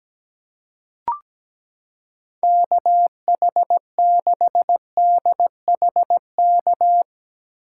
Morse code, heard as a keyed tone, about 17 wpm, 700 Hz. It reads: KH6DHK
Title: K1USN SST QSO Practice - 17wpm